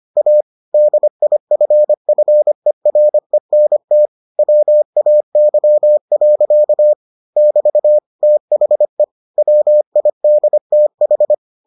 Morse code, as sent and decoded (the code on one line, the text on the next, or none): .- / -.. .. ..-. ..-. .-. -. - / .-- .- -.-- .-.-.- / -...- / - .... / .-- .. -.. - ....
A DIFFERENT WAY. <BT> THE WIDTH